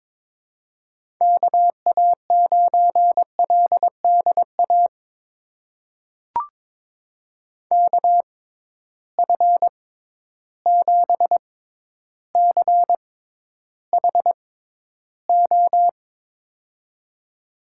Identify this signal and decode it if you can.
Morse code, heard as a keyed tone, about 21 wpm, spaced as for 11 wpm, 700 Hz. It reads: KA9LDA KF7CHO
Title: USA Call Signs - 6 Characters - 6x Character Spacing - 22wpm